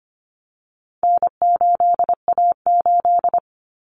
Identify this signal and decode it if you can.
Morse code, heard as a keyed tone, about 25 wpm, 700 Hz. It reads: N8A8
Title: Common Cut Numbers - 4-Digit 25wpm